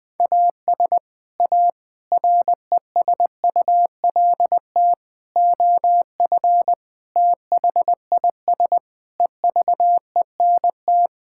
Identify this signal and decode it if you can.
Morse code, heard as a keyed tone, about 20 wpm, 700 Hz. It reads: AS A RESULT OF THIS EVENT
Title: Sentences with 5-Word Phrases - 20wpm